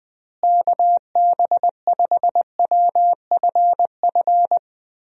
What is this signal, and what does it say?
Morse code, heard as a keyed tone, about 20 wpm, 700 Hz. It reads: KB5WFF
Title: USA Call Signs - 6 Characters - Repeated 3x - 20wpm